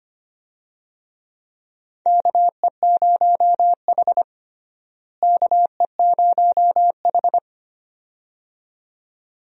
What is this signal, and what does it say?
Morse code, heard as a keyed tone, about 25 wpm, 700 Hz. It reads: KE0H KE0H
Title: USA Call Signs - 4 Characters - Repeated 2x - 25wpm